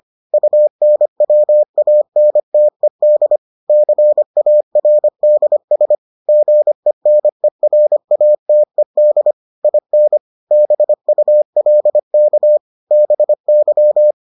Morse code, heard as a keyed tone, about 25 wpm, 600 Hz. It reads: UNWANTED CARDS GENERATED IN BULK BY